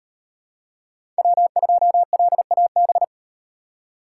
Morse code, heard as a keyed tone, about 38 wpm, 700 Hz. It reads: W2LAB